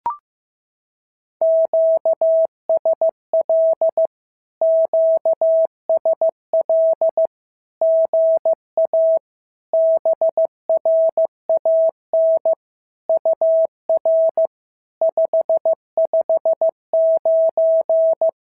Morse code, heard as a keyed tone, about 15 wpm, 650 Hz. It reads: QSL QSL GA BRAN UR 559
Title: SOTA QSO Practice 15wpm